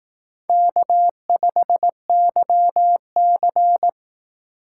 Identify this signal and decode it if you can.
Morse code, heard as a keyed tone, about 18 wpm, 700 Hz. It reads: K5YC
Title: Speed Racing - USA 1x2 Call Signs 15wpm